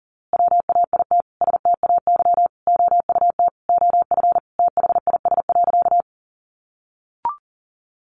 Morse code, heard as a keyed tone, about 40 wpm, 700 Hz. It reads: WAIT STAY OUT OF THIS.
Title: A Sound Of Thunder 40wpm